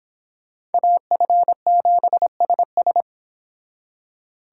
Morse code, heard as a keyed tone, about 28 wpm, 700 Hz. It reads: AF7SS